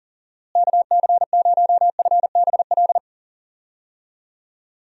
Morse code, heard as a keyed tone, about 40 wpm, 700 Hz. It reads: KC0FBL
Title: USA Call Signs - 6 Characters - Repeated 2x - 40wpm